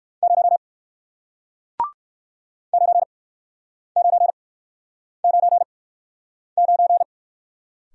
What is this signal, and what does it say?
Morse code, heard as a keyed tone, about 56 wpm, spaced as for 9 wpm, 700 Hz. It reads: 0 9999
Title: Speed-Racing Single Letter-Number 45wpm